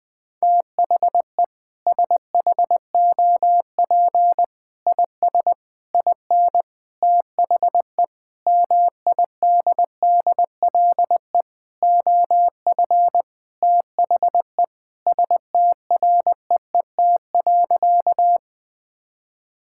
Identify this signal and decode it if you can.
Morse code, heard as a keyed tone, about 20 wpm, 700 Hz. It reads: THE SHOP IS IN THE MIDDLE OF THE STREET.
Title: Sentences with 4-Word Phrases - Spoken Beginning - 20wpm